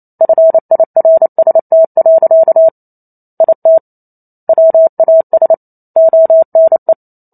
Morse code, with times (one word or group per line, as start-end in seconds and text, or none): 0.2-2.7 FIRST.
3.4-3.8 IT
4.5-5.5 WAS
6.0-6.9 ONE